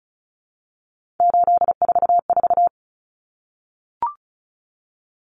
Morse code, heard as a keyed tone, about 34 wpm, 700 Hz. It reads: 844
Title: Common Cut Numbers - 3-Digit 35wpm